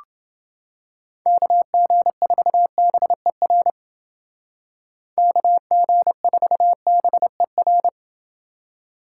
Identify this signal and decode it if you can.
Morse code, heard as a keyed tone, about 28 wpm, 700 Hz. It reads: KG4BER KG4BER